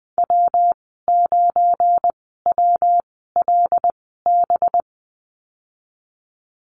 Morse code, heard as a keyed tone, about 19 wpm, spaced as for 14 wpm, 700 Hz. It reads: W9WLB
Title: 186 Lesson 30g - Letter b - Call Signs 20wpm